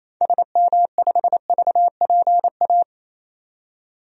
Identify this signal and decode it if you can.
Morse code, heard as a keyed tone, about 28 wpm, 700 Hz. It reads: SM5VPA
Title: DXCC-100 Call Signs - 6 Characters - 28wpm